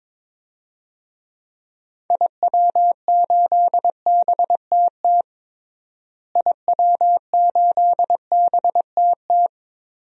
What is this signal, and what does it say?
Morse code, heard as a keyed tone, about 21 wpm, 700 Hz. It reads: IW8BTT IW8BTT
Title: DXCC-100 Call Signs - 6 Characters - Repeated 3x - 22wpm